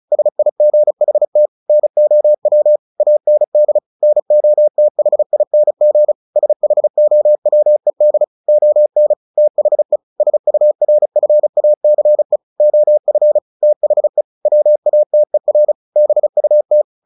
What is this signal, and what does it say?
Morse code, heard as a keyed tone, about 35 wpm, 600 Hz. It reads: SIGHT NOW AND NOTHING SHOWED ON THE SURFACE OF THE WATER BUT
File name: the old man and the sea / the old man and the sea 1 35wpm